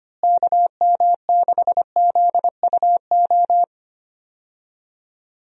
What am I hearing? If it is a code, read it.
Morse code, heard as a keyed tone, about 25 wpm, 700 Hz. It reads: KM6ZUO
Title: USA Call Signs - 6 Characters - Repeated 2x - 25wpm